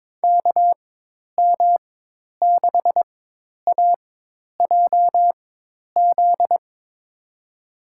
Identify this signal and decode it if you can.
Morse code, heard as a keyed tone, about 22 wpm, spaced as for 10 wpm, 700 Hz. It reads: KM6AJZ
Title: USA Call Signs - 6 Characters - 4x Character Spacing - 22wpm